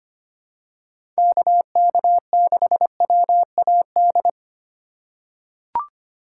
Morse code, heard as a keyed tone, about 25 wpm, 700 Hz. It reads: KK6WAD
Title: USA Call Signs - 6 Characters - 4x Character Spacing - 25wpm